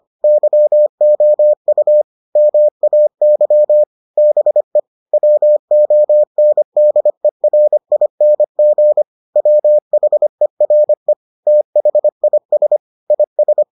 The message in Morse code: YOU MAY BE WONDERING WHERE THIS IS

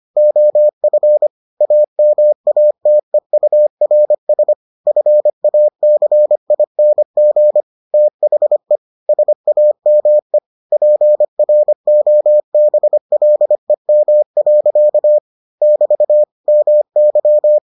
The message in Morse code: OF AMATEURS FACING THE SAME PROBLEM. <BT> MY

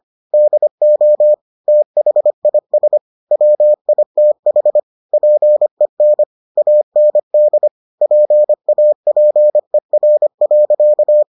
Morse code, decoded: DO THIS WITH PEN AND PAPER.

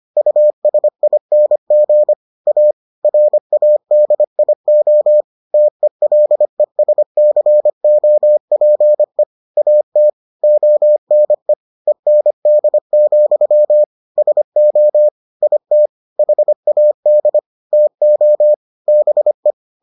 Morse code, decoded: USING A RADIO TELESCOPE AT ONE END, SO IT HAD TO BE

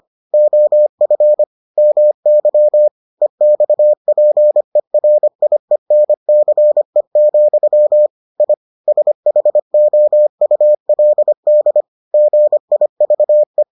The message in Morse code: OF MY EXPERIENCE, I SHOULD GIVE

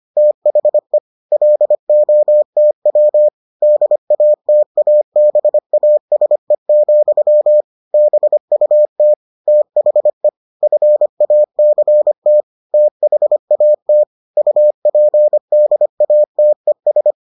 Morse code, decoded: THE LOTW DATABASE, BUT THE FACT THAT UPDATES

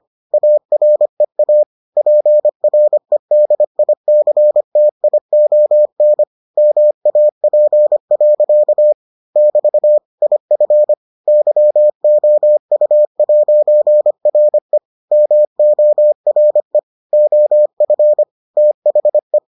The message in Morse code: AREA PREDICTION MAP. <BT> IF YOU'RE MORE OF THE

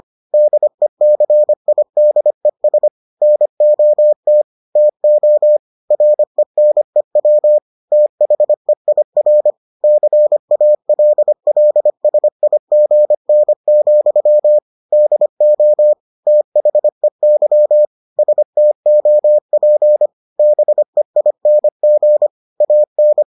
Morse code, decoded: DECIDES NOT TO RENEW THEIR CALLSIGN, DO THEY STOP BEING AN

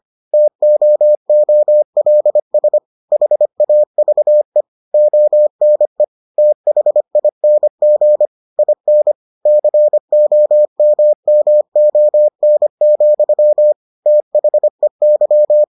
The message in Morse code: TOOLS HAVE ONE THING IN COMMON, THEY

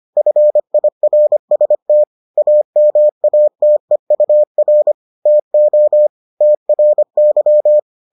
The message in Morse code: FIRST AMATEUR TO TRY

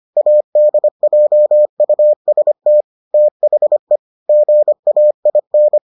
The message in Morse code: ADJUST THE GAIN

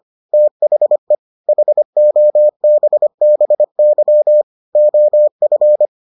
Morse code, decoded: THE HOBBY OF

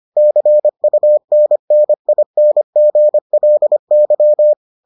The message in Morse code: CUNNINGLY

